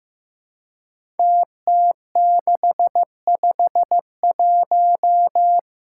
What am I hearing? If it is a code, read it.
Morse code, heard as a keyed tone, about 15 wpm, 700 Hz. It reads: TT651
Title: Five Challenging Letter-Number Combinations 15wpm